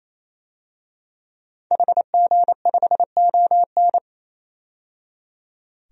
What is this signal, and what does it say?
Morse code, heard as a keyed tone, about 28 wpm, 700 Hz. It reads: HG5ON
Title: DXCC-100 Call Signs - 5 Characters - 28wpm